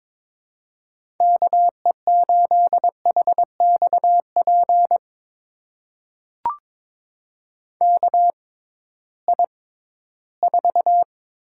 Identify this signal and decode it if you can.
Morse code, heard as a keyed tone, about 22 wpm, spaced as for 14 wpm, 700 Hz. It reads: KE8HXP KI4
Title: USA Call Signs - 6 Characters - 6x Character Spacing - 22wpm